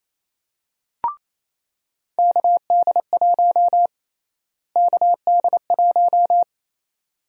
Morse code, heard as a keyed tone, about 27 wpm, 700 Hz. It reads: KD1 KD1
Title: USA Call Sign Prefixes - Repeated 2x - 28wpm